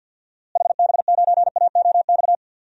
Morse code, heard as a keyed tone, about 52 wpm, 700 Hz. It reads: HB9AOX
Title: DXCC-100 Call Signs - 6 Characters - Repeated 2x - 50wpm